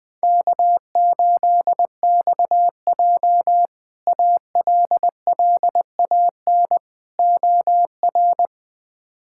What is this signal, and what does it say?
Morse code, heard as a keyed tone, about 20 wpm, 700 Hz. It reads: K8XJ ALLAN OR
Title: K1USN SST QSO Practice - 20wpm